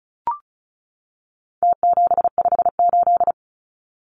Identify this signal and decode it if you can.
Morse code, heard as a keyed tone, about 34 wpm, 700 Hz. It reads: T758
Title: Numbers - Contest Style - 4-Digit 35wpm